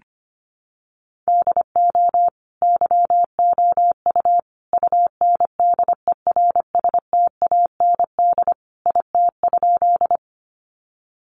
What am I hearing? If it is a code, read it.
Morse code, heard as a keyed tone, about 25 wpm, 700 Hz. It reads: DO YOU UNDERSTAND IT?
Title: Sentences from Top 600 Words 25wpm